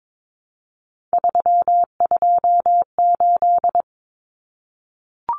Morse code, heard as a keyed tone, about 22 wpm, 700 Hz. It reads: 328
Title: Common Cut Numbers - 3-Digit 22wpm